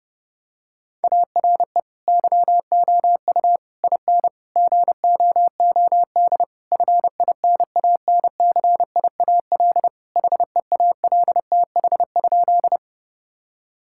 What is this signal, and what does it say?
Morse code, heard as a keyed tone, about 32 wpm, 700 Hz. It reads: ARE YOU IN GOOD FINANCIAL HEALTH?
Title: Sentences from Top 1000 Words - Review Part 5 - 30wpm